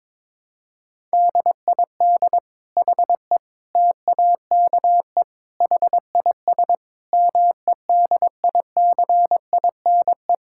Morse code, heard as a keyed tone, about 22 wpm, 700 Hz. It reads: DID HE TAKE HIS MEDICINE
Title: Sentences from Top 1000 Words - Review Part 6 - 22wpm